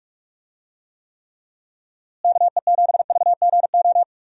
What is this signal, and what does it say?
Morse code, heard as a keyed tone, about 41 wpm, 700 Hz. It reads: KE7VGO
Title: USA Call Signs - 6 Characters - Repeated 3x - 45wpm